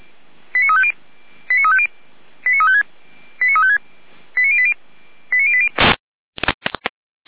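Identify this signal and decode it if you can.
Five-tone selective call (ZVEI), heard as quick runs of five tones, tones of about 70 ms. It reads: 89350 89350 89356 89356 89080 89080
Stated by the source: ZVEI five-tone selective call 89401 received by SDR.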